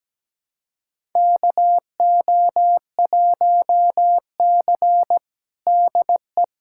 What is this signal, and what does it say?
Morse code, heard as a keyed tone, about 17 wpm, 700 Hz. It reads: KO1C DE
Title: USA Callsign DE Callsign - 4 Characters - 17wpm